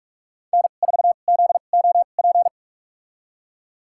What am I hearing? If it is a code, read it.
Morse code, heard as a keyed tone, about 42 wpm, spaced as for 31 wpm, 700 Hz. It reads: N4ZOP
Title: USA Call Signs - 5 Characters - 2x Character Spacing - 45wpm